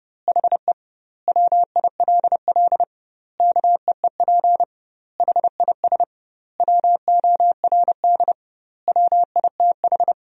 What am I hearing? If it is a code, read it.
Morse code, heard as a keyed tone, about 32 wpm, 700 Hz. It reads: HE WILL KEEP HIS WORD WITH
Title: 2x Word-Spacing - Sentences from Top 300 Words - 30wpm